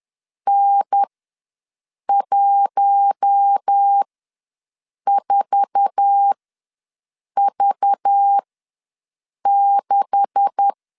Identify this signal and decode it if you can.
Morse code, heard as a keyed tone, about 10 wpm, spaced as for 6 wpm, 800 Hz. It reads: N14V6
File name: LEZIONI / 9 286 ws 10wpm 4032